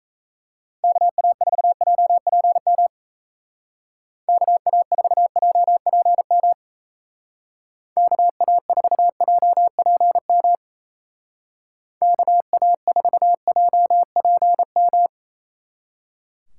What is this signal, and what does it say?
Morse code, heard as a keyed tone, about 36 wpm, 700 Hz. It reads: KA4JPM KA4JPM KA4JPM KA4JPM